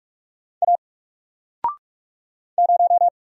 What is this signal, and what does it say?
Morse code, heard as a keyed tone, about 44 wpm, 700 Hz. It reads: A 0
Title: Single Letter-Number Alt 45wpm